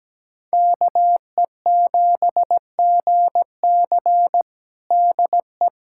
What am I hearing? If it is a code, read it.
Morse code, heard as a keyed tone, about 17 wpm, 700 Hz. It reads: KE7GC DE
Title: USA Callsign DE Callsign 17wpm